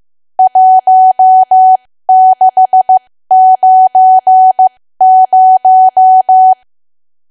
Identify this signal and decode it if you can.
Morse code, heard as a keyed tone, about 15 wpm, spaced as for 13 wpm, 750 Hz. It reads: 1690
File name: cwa / W1AW-Level2 / 13WPM